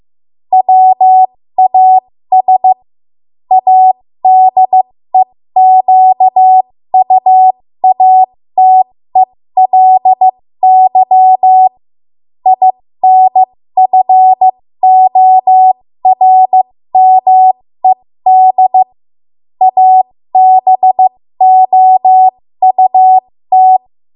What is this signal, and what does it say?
Morse code, heard as a keyed tone, about 15 wpm, spaced as for 13 wpm, 750 Hz. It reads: WAS ADEQUATELY INFORMED ABOUT
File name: Morse / 13WPM